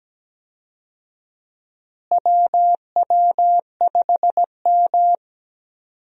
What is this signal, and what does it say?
Morse code, heard as a keyed tone, about 17 wpm, 700 Hz. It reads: WW5M
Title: USA Call Signs - 4 Characters - Repeated 2x - 17wpm